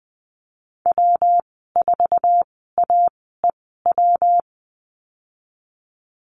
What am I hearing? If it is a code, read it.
Morse code, heard as a keyed tone, about 20 wpm, spaced as for 15 wpm, 700 Hz. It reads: W4AEW
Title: 167 Lesson 27g - Number 7 - Call Signs 20wpm